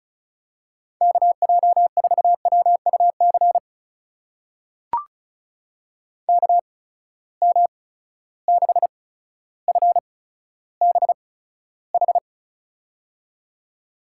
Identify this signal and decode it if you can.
Morse code, heard as a keyed tone, about 36 wpm, spaced as for 15 wpm, 700 Hz. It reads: KJ4WUC KM6FBH
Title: USA Call Signs - 6 Characters - 8x Character Spacing - 35wpm